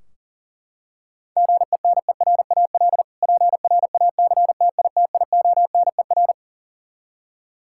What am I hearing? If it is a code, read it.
Morse code, heard as a keyed tone, about 40 wpm, 690 Hz. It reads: GENERAL PRACTITIONER
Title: Practice QSO - Exchange 3 - More Info - 40wpm